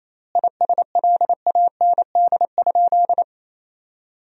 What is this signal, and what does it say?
Morse code, heard as a keyed tone, about 28 wpm, 700 Hz. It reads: ISLAND?
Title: 3x Word-Spacing - Sentences from Top 500 Words - 28wpm